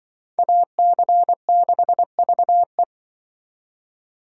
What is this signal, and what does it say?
Morse code, heard as a keyed tone, about 24 wpm, 700 Hz. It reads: AC6VE
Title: Speed Racing - USA 2x2 Call Signs 20wpm